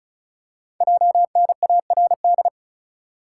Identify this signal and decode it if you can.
Morse code, heard as a keyed tone, about 36 wpm, 700 Hz. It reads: JNARD
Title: Code Groups - 5 Letters - 35wpm